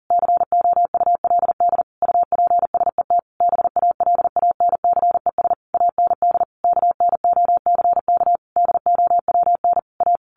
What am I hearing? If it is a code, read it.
Morse code, heard as a keyed tone, about 40 wpm, 700 Hz. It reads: COULD UPSET BALANCES AND KNOCK DOWN A
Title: A Sound Of Thunder 40wpm